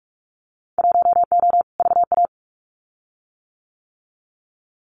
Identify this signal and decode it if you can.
Morse code, heard as a keyed tone, about 44 wpm, 700 Hz. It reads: 1O VA
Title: Winter Field Day Exchanges 45wpm